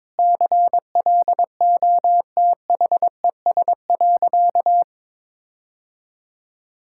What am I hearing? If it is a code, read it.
Morse code, heard as a keyed tone, about 22 wpm, 690 Hz. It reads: CLOTHES.
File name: practice / 22wpm / Easy Sentences with Hints - 22wpm